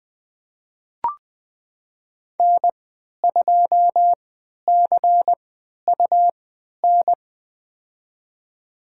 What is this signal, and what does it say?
Morse code, heard as a keyed tone, about 20 wpm, spaced as for 11 wpm, 700 Hz. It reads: N2CUN
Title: USA Call Signs - 5 Characters - 3x Character Spacing - 20wpm